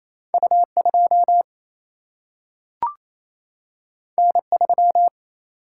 Morse code, heard as a keyed tone, about 28 wpm, 700 Hz. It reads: U 2 N 3
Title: Single-Letter Single-Number 28wpm